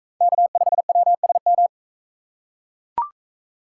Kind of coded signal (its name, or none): Morse code